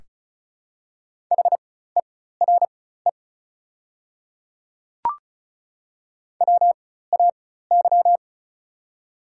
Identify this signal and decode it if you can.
Morse code, heard as a keyed tone, about 36 wpm, spaced as for 16 wpm, 700 Hz. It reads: HERE WAY
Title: 4x Character Spacing - Top 200 Words - 35wpm